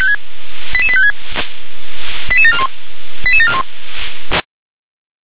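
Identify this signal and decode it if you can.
Five-tone selective call (ZVEI), heard as quick runs of five tones, tones of about 70 ms. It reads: 88856 88741 88741
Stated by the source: ZVEI five-tone selective call 88773 received with an SDR.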